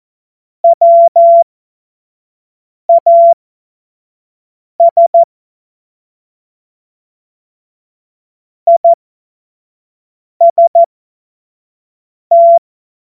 Morse code, heard as a keyed tone, about 14 wpm, spaced as for 5 wpm, 680 Hz. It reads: WAS IST
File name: Morsekurs / Witze / Witz 02. August 2022 25 F70